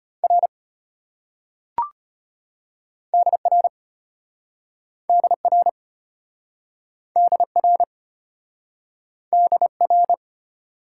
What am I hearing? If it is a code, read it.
Morse code, heard as a keyed tone, about 33 wpm, 700 Hz. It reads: R DR DR DR DR